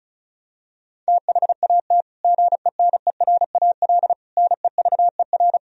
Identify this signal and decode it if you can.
Morse code, heard as a keyed tone, about 36 wpm, 700 Hz. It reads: THAT GENERAL NEVER
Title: Sentences from Top 1000 Words - Review Part 4 - 35wpm